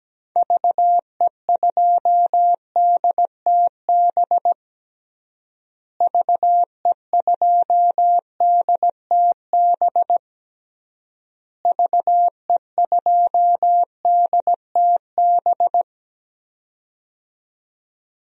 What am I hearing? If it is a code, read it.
Morse code, heard as a keyed tone, about 17 wpm, 700 Hz. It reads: VE2DTB VE2DTB VE2DTB
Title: DXCC-100 Call Signs - 6 Characters - Repeated 3x - 17wpm